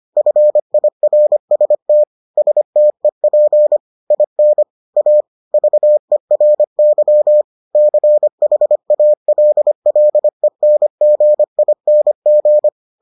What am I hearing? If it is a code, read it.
Morse code, heard as a keyed tone, about 25 wpm, 600 Hz. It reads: FIRST STEP IN A VERY CHALLENGING